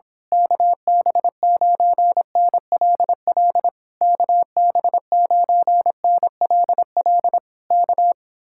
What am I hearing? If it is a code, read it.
Morse code, heard as a keyed tone, about 26 wpm, 700 Hz. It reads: KB9NLL KB9NLL K
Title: Speed-Racing Common QSO Phrases - 22wpm